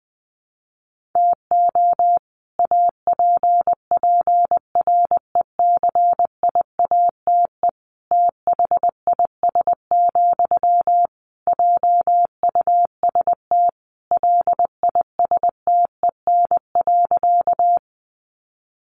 Morse code, heard as a keyed tone, about 20 wpm, 700 Hz. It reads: TO APPRECIATE THIS, JUST LISTEN.
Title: Sentences from Top 1300 Words 20wpm